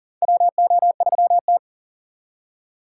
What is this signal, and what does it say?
Morse code, heard as a keyed tone, about 40 wpm, 700 Hz. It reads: WO3T
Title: USA Call Signs - 4 Characters - 6x Character Spacing - 40wpm